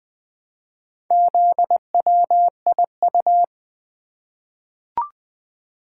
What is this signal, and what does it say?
Morse code, heard as a keyed tone, about 20 wpm, 700 Hz. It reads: ZWIU
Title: Four Letter-Number Combinations - 20wpm